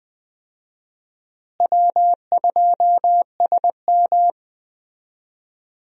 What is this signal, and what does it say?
Morse code, heard as a keyed tone, about 20 wpm, 700 Hz. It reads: W2SM
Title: USA Call Signs - 4 Characters - Repeated 3x - 20wpm